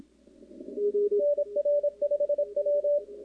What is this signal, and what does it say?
Morse code, heard as a keyed tone, about 26 wpm, 580 Hz. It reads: NR5W
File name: contests / 2019-cw-wpx-kt5j-k5pi- -2330 0-11